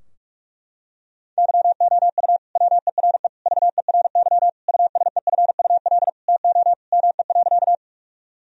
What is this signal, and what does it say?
Morse code, heard as a keyed tone, about 46 wpm, 700 Hz. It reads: YOU WERE VERY USEFUL TO ME.
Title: Sentences from Top 1000 Words - Review Part 4 - 45wpm